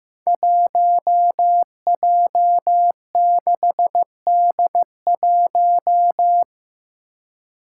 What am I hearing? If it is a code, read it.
Morse code, heard as a keyed tone, about 15 wpm, 700 Hz. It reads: 1J6D1
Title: Five Challenging Letter-Number Combinations 15wpm